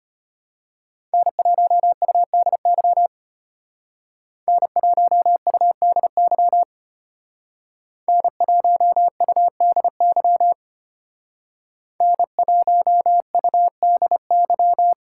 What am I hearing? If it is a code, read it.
Morse code, heard as a keyed tone, about 32 wpm, 700 Hz. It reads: N1UDY N1UDY N1UDY N1UDY